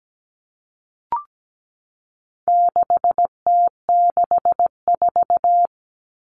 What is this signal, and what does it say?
Morse code, heard as a keyed tone, about 17 wpm, 700 Hz. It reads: 6T64
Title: All Cut Numbers - 4-Digit 17wpm